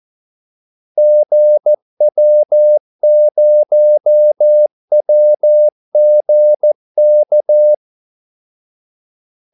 Morse code, heard as a keyed tone, about 14 wpm, 600 Hz. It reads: GW0WGK
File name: Callsigns-14wpm